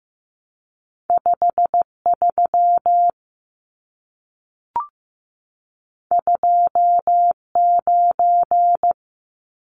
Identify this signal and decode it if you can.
Morse code, heard as a keyed tone, about 15 wpm, 700 Hz. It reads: 53 29
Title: Numbers - 2-Digit 15wpm